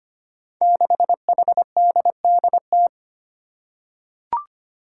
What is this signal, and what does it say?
Morse code, heard as a keyed tone, about 25 wpm, 700 Hz. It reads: 6HDDT
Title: Five Challenging Letter-Number Combinations 25wpm